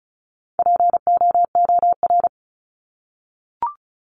Morse code, heard as a keyed tone, about 34 wpm, 700 Hz. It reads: POOR
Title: Top 751-1000 Words 35wpm